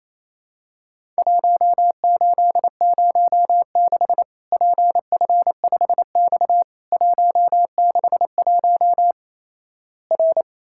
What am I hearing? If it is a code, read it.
Morse code, heard as a keyed tone, about 28 wpm, 700 Hz. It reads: 1806 PF5X 161 R
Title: Worked All Europe DX Contest - QTC Exchange - 28wpm